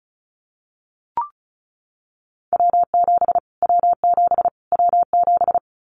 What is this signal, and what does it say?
Morse code, heard as a keyed tone, about 34 wpm, 700 Hz. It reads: W7 W7 W7
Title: Callsign Prefixes for Top 25 DXCC - Repeated 3x - 35wpm